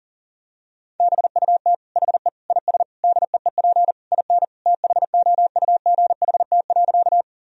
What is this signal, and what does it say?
Morse code, heard as a keyed tone, about 40 wpm, 700 Hz. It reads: BUT HE IS DEEP IN THOUGHT.
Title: Sentences from Top 1000 Words - Review Part 3 - 40wpm